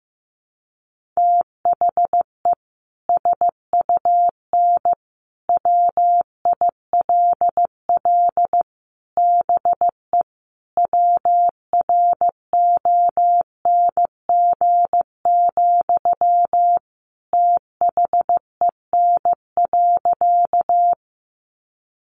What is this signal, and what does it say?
Morse code, heard as a keyed tone, about 15 wpm, 700 Hz. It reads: THE SUN WILL BE WRONG, THEN.
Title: Sentences from Top 900 Words 15wpm